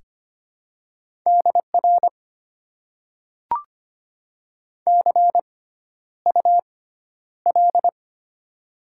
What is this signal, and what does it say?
Morse code, heard as a keyed tone, about 25 wpm, spaced as for 11 wpm, 700 Hz. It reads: DR CUL
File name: Top 90 QSO Elements - 6x Character Spacing - 25wpm